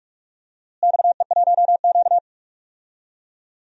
Morse code, heard as a keyed tone, about 46 wpm, 700 Hz. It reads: XE1Q
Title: DXCC-100 Call Signs - 4 Characters - Repeated 2x - 45wpm